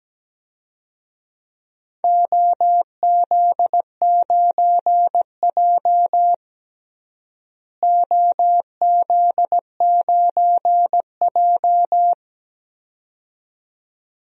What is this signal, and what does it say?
Morse code, heard as a keyed tone, about 17 wpm, 700 Hz. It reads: OZ9J OZ9J
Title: DXCC-100 Call Signs - 4 Characters - Repeated 2x - 17wpm